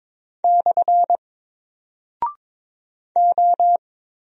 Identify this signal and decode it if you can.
Morse code, heard as a keyed tone, about 22 wpm, 700 Hz. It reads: / O
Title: Single Character (Letter:Number:Punctuation:Prosign) 22wpm